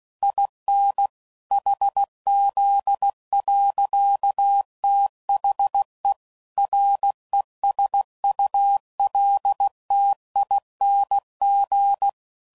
Morse code, heard as a keyed tone, about 16 wpm, 800 Hz. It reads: IN HZ. THE RESULTING
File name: CW Code Groups @15WPM / Technical Subjects / XC